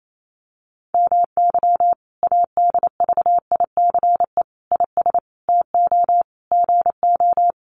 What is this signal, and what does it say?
Morse code, heard as a keyed tone, about 28 wpm, 700 Hz. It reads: MY ADVICE IS TO GO
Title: Sentences from Top 1300 Words 28wpm